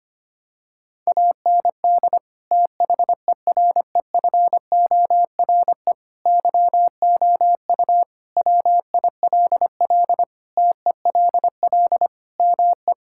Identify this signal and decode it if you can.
Morse code, heard as a keyed tone, about 25 wpm, 700 Hz. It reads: AND THEREFORE YOU WILL TELL ME
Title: Sentences from Top 1000 Words - Review Part 5 - 25wpm